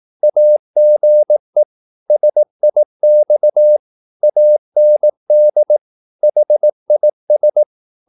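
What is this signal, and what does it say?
Morse code, heard as a keyed tone, about 18 wpm, 600 Hz. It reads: AGE SIX AND HIS
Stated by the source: Morse practice at 18 wpm, English text, from CW med Gnister.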